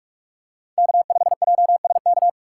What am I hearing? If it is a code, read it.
Morse code, heard as a keyed tone, about 44 wpm, 700 Hz. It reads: K5JSK